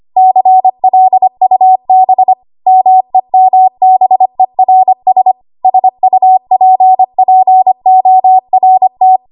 Morse code, heard as a keyed tone, about 25 wpm, 750 Hz. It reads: CLUB MEMBERS SUPPORT
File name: Morse / 25WPM